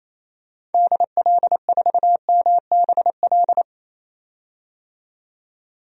Morse code, heard as a keyed tone, about 28 wpm, 700 Hz. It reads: DL4MBL
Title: DXCC-100 Call Signs - 6 Characters - 28wpm